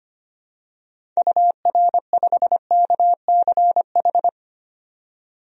Morse code, heard as a keyed tone, about 25 wpm, 690 Hz. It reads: UR5KCH
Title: DXCC-100 Call Signs - 6 Characters - 25wpm